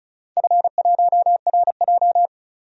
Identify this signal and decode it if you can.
Morse code, heard as a keyed tone, about 36 wpm, 700 Hz. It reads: F1RJ
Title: DXCC-100 Call Signs - 4 Characters - 35wpm